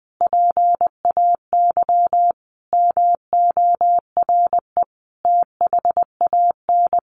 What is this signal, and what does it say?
Morse code, heard as a keyed tone, about 20 wpm, 700 Hz. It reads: PAY MORE THAN
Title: Sentences from Top 1100 Words 20wpm